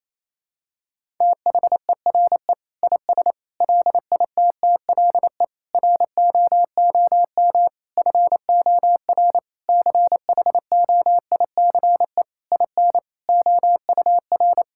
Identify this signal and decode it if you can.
Morse code, heard as a keyed tone, about 28 wpm, 700 Hz. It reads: THERE IS LITTLE ROOM FOR CHOICE IN OUR
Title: Sentences from Top 1000 Words - Review Part 6 - 28wpm